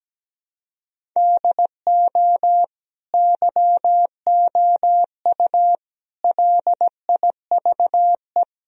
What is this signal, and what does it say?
Morse code, heard as a keyed tone, about 17 wpm, 700 Hz. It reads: DO YOU LIVE
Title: Sentences from Top 1000 Words - Review Part 1 - 17wpm